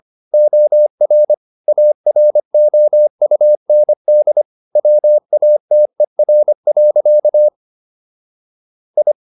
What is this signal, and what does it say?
Morse code, heard as a keyed tone, about 25 wpm, 600 Hz. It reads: OR AROUND WATER. I